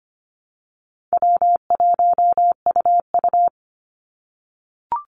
Morse code, heard as a keyed tone, about 25 wpm, 700 Hz. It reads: W1UU
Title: Easy Call Signs - 25wpm